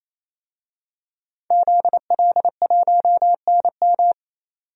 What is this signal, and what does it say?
Morse code, heard as a keyed tone, about 27 wpm, 700 Hz. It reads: ZL1NM
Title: DXCC-100 Call Signs - 5 Characters - 28wpm